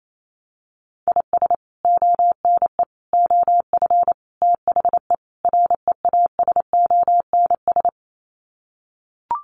Morse code, heard as a keyed tone, about 28 wpm, 700 Hz. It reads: IS ONE OF THE REASONS
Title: Sets of 5 Words Encore2 28wpm